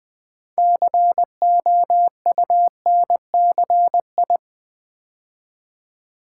Morse code, heard as a keyed tone, about 20 wpm, 700 Hz. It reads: COUNCI
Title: Build A Word - 3 Letters At A Time - 9-Letter Words - 20wpm